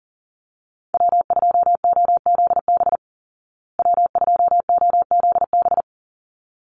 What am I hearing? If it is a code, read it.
Morse code, heard as a keyed tone, about 40 wpm, 700 Hz. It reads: W2OZB W2OZB
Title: Call Signs Repeated 3x - 40wpm